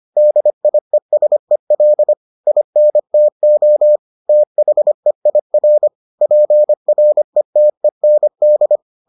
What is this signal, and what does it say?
Morse code, heard as a keyed tone, about 25 wpm, 600 Hz. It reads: DIESEL INTO THEIR PRETEND